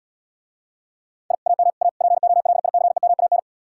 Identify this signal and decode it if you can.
Morse code, heard as a keyed tone, about 77 wpm, 700 Hz. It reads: I AM A POLICEMAN